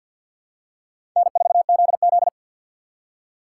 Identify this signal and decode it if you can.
Morse code, heard as a keyed tone, about 50 wpm, 700 Hz. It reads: N4ZZ
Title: K1USN SST QSO Practice - 50wpm